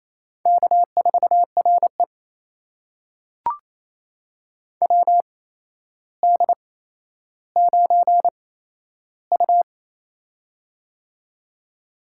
Morse code, heard as a keyed tone, about 27 wpm, spaced as for 12 wpm, 700 Hz. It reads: K4RE WD9U
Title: USA Call Signs - 4 Characters - 8x Character Spacing - 28wpm